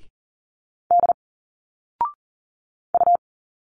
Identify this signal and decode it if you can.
Morse code, heard as a keyed tone, about 40 wpm, 700 Hz. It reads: D U
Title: Single Letters 40wpm